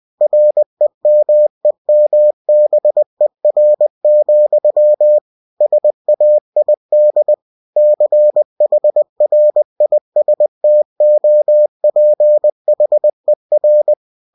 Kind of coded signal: Morse code